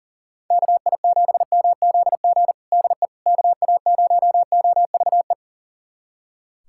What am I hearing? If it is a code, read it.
Morse code, heard as a keyed tone, about 40 wpm, 700 Hz. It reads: KI7MZG DE KA0OVE
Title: USA Callsign DE Callsign 40wpm